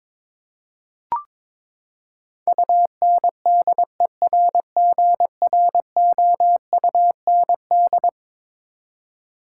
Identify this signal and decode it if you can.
Morse code, heard as a keyed tone, about 22 wpm, 700 Hz. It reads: UNDERGROUND
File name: practice / 22wpm / Eleven Letter Words - 22wpm